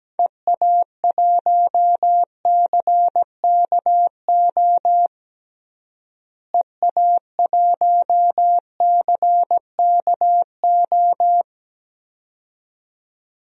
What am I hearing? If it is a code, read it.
Morse code, heard as a keyed tone, about 17 wpm, 700 Hz. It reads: EA1CKO EA1CKO
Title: DXCC-100 Call Signs - 6 Characters - Repeated 2x - 17wpm